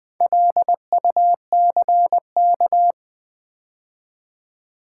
Morse code, heard as a keyed tone, about 20 wpm, 700 Hz. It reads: LUCK
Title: Build A Word - 2 Letters At A Time - 4-Letter Words - 20wpm